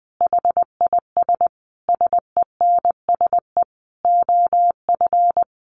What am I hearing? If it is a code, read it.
Morse code, heard as a keyed tone, about 20 wpm, 700 Hz. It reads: HIS SENSE OF
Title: Sentences from Top 1400 Words 20wpm